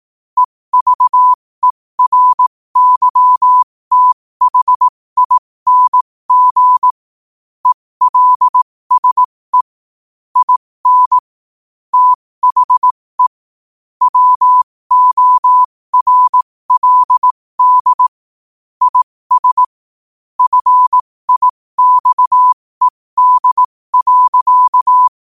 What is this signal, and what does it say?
Morse code, heard as a keyed tone, about 18 wpm, spaced as for 15 wpm, 1000 Hz. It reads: EVERYTHING ELSE IN THE WORLD IS FIXED.
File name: qotdfiles / 15wpm